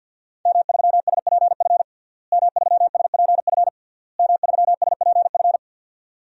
Morse code, heard as a keyed tone, about 52 wpm, 700 Hz. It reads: M3SPF M3SPF M3SPF
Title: DXCC-100 Call Signs - 5 Characters - Repeated 3x - 50wpm